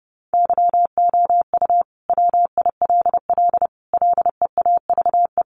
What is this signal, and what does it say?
Morse code, heard as a keyed tone, about 28 wpm, 700 Hz. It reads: YOU WILL LEAVE
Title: Sentences from Top 900 Words 30wpm